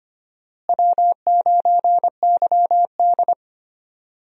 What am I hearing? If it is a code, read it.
Morse code, heard as a keyed tone, about 25 wpm, 700 Hz. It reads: W9YD
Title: USA Call Signs - 4 Characters - Repeated 3x - 25wpm